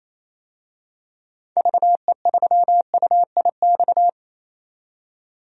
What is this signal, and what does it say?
Morse code, heard as a keyed tone, about 28 wpm, 700 Hz. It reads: VE3UIX
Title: DXCC-100 Call Signs - 6 Characters - 28wpm